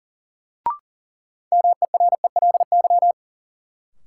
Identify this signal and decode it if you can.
Morse code, heard as a keyed tone, about 40 wpm, 700 Hz. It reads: MERELY
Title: Sentences from Top 1400 Words - New Word Drill Alt - 40wpm